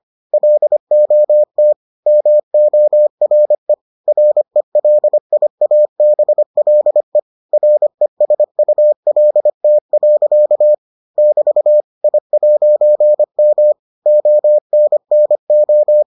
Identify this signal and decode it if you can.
Morse code, heard as a keyed tone, about 25 wpm, 600 Hz. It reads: LOT MORE RELIABLE RESULT. <BT> I'M ONNO